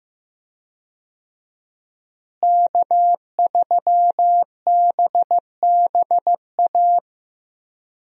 Morse code, heard as a keyed tone, about 15 wpm, 700 Hz. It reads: K3BBA
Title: USA Call Signs - 5 Characters - Repeated 3x - 15wpm